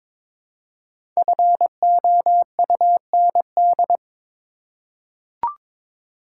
Morse code, heard as a keyed tone, about 22 wpm, 700 Hz. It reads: FOUND
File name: 5 Letter Words - First 3 Letters Given - 22wpm